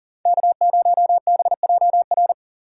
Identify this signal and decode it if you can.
Morse code, heard as a keyed tone, about 37 wpm, 700 Hz. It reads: K0BJR
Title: USA Call Signs - 5 Characters - 4x Character Spacing - 40wpm